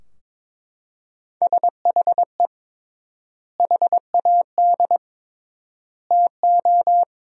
Morse code, heard as a keyed tone, about 22 wpm, 700 Hz. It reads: SHE HAD TO
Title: 3x Word-Spacing - Sentences from Top 200 Words - 22wpm